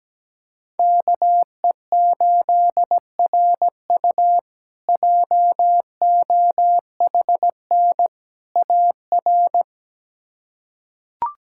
Morse code, heard as a keyed tone, about 17 wpm, 700 Hz. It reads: KE8RU JOHN AR
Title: K1USN SST QSO Practice - 17wpm